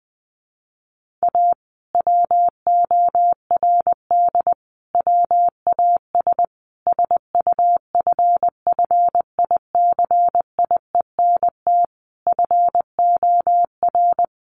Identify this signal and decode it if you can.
Morse code, heard as a keyed tone, about 19 wpm, 700 Hz. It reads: A WORD WAS SUFFICIENT FOR
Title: Sentences from Top 1400 Words 20wpm